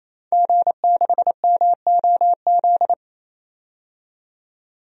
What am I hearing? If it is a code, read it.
Morse code, heard as a keyed tone, about 28 wpm, 700 Hz. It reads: G6MOZ
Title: DXCC-100 Call Signs - 5 Characters - 28wpm